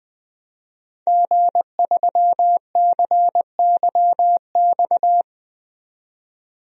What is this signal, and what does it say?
Morse code, heard as a keyed tone, about 20 wpm, 700 Hz. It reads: G3CYX
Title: DXCC-100 Call Signs - 5 Characters - 20wpm